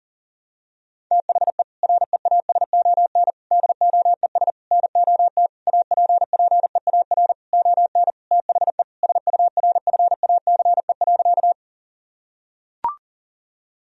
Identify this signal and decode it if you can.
Morse code, heard as a keyed tone, about 37 wpm, 700 Hz. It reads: THE REASON DOES NOT APPEAR ON THE SURFACE.
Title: Sentences from Top 1000 Words - Review Part 3 - 40wpm